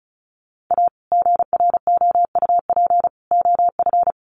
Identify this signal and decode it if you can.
Morse code, heard as a keyed tone, about 34 wpm, 700 Hz. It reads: A GROUP OF
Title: Sets of 3 Words 35wpm